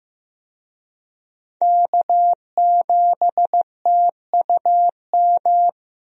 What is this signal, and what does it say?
Morse code, heard as a keyed tone, about 15 wpm, 700 Hz. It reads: K7TUM
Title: USA Call Signs - 5 Characters - 15wpm